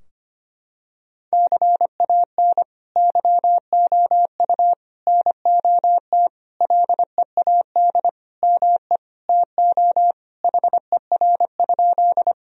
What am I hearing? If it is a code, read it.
Morse code, heard as a keyed tone, about 25 wpm, 700 Hz. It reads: CAN YOU NOT LEAD ME TO HER?
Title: Sentences from Top 1000 Words - Review Part 2 - 25wpm